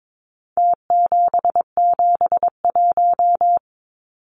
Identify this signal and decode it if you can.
Morse code, heard as a keyed tone, about 22 wpm, 700 Hz. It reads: T771
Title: Numbers - Contest Style - 4-Digit 22wpm